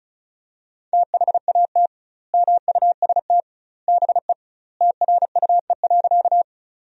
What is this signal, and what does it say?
Morse code, heard as a keyed tone, about 36 wpm, 700 Hz. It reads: THAT MUST BE TRUE.
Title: 2x Word-Spacing - Sentences from Top 400 Words - 35wpm